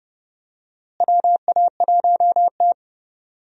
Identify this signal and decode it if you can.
Morse code, heard as a keyed tone, about 30 wpm, 700 Hz. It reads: WA1T
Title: USA Call Signs - 4 Characters - Repeated 3x - 30wpm